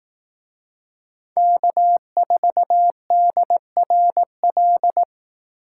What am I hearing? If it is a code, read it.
Morse code, heard as a keyed tone, about 18 wpm, 700 Hz. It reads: K4DRL
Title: Speed Racing - USA 1x3 Call Signs 15wpm